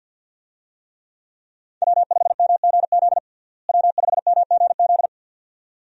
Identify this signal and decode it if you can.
Morse code, heard as a keyed tone, about 52 wpm, 700 Hz. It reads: W5MGZ W5MGZ
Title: USA Call Signs - 5 Characters - Repeated 2x - 50wpm